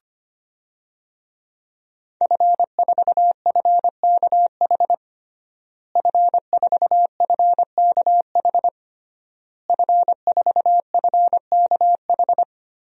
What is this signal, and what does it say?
Morse code, heard as a keyed tone, about 25 wpm, 700 Hz. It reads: F4FKH F4FKH F4FKH
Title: DXCC-100 Call Signs - 5 Characters - Repeated 3x - 25wpm